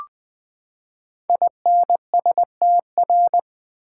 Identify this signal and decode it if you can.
Morse code, heard as a keyed tone, about 20 wpm, 700 Hz. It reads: INSTR
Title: Build A Word - 5 Letters At A Time - 10-Letter Words - 20wpm